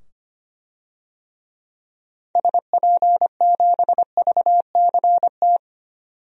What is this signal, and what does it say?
Morse code, heard as a keyed tone, about 25 wpm, 700 Hz. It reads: SP7VCT
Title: DXCC-100 Call Signs - 6 Characters - 25wpm